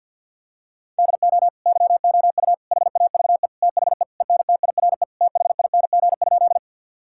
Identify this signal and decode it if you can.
Morse code, heard as a keyed tone, about 51 wpm, 700 Hz. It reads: DO YOU HAVE THE ENTIRE THING?
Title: Sentences from Top 1000 Words - Review Part 5 - 50wpm